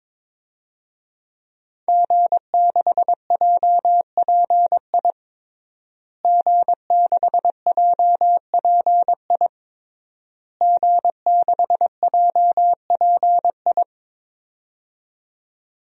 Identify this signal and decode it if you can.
Morse code, heard as a keyed tone, about 22 wpm, 700 Hz. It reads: G6JPI G6JPI G6JPI
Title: DXCC-100 Call Signs - 5 Characters - Repeated 3x - 22wpm